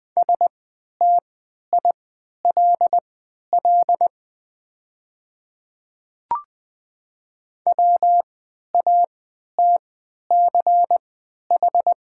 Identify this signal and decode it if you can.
Morse code, heard as a keyed tone, about 20 wpm, spaced as for 11 wpm, 690 Hz. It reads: STILL WATCH
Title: 3x Character Spacing - Top 500 Words - 20wpm